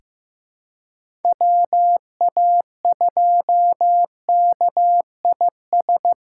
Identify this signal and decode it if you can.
Morse code, heard as a keyed tone, about 16 wpm, 700 Hz. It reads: WA2KIS
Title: USA Callsign DE Callsign 15wpm